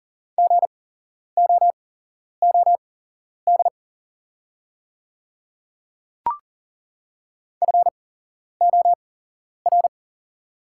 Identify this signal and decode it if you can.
Morse code, heard as a keyed tone, about 37 wpm, spaced as for 11 wpm, 690 Hz. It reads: GOOD FOR
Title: 8x Character Spacing - Top 200 Words - 40wpm